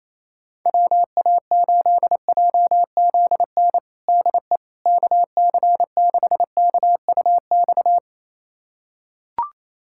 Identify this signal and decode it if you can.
Morse code, heard as a keyed tone, about 28 wpm, 700 Hz. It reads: WA8JZN DE KC6KUX
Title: USA Callsign DE Callsign - 6 Characters - 28wpm